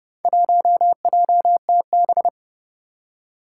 Morse code, heard as a keyed tone, about 30 wpm, 700 Hz. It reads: 1JTB
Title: Four Challenging Letter-Number Combinations 30wpm